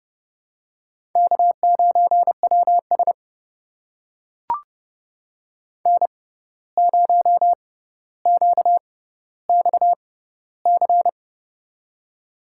Morse code, heard as a keyed tone, about 32 wpm, spaced as for 15 wpm, 700 Hz. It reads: K9WS N0QXC
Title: USA Call Signs - 6x Character Spacing - 30wpm